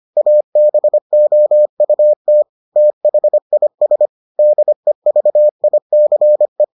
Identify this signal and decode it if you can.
Morse code, heard as a keyed tone, about 25 wpm, 600 Hz. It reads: ABOUT THIS DEVICE